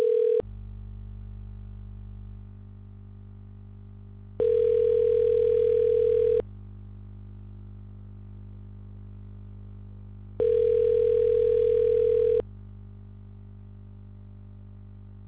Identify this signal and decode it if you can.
Ringback tone (North America), 440 + 480 Hz.